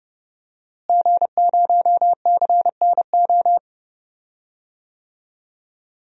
Morse code, heard as a keyed tone, about 30 wpm, 700 Hz. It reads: G0CNO
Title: DXCC-100 Call Signs - 5 Characters - Repeated 3x - 30wpm